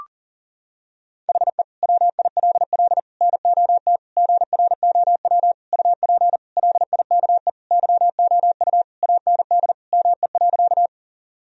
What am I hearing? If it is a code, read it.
Morse code, heard as a keyed tone, about 37 wpm, 700 Hz. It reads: HE WILL NOT GROW UP LIKE YOU AND ME.
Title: Sentences from Top 1000 Words - Review Part 1 - 40wpm